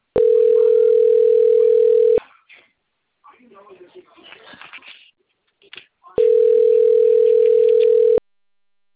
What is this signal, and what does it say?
Ringback tone (North America), 440 + 480 Hz, with speech.
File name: rg-1600-unknown- - - .37698